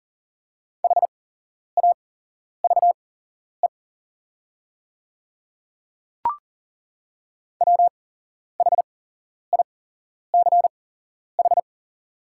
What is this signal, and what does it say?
Morse code, heard as a keyed tone, about 40 wpm, spaced as for 11 wpm, 700 Hz. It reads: HAVE WHICH
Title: 8x Character Spacing - Top 100 Words - 40wpm